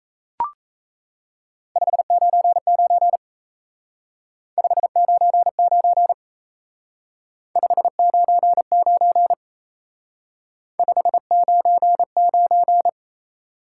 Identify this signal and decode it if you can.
Morse code, heard as a keyed tone, about 35 wpm, 700 Hz. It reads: 599 599 599 599
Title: Speed-Racing Top 180 QSO Elements 28wpm